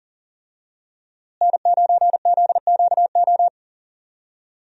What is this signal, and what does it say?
Morse code, heard as a keyed tone, about 40 wpm, 700 Hz. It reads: N9ZQO
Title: USA Call Signs - 5 Characters - Repeated 2x - 40wpm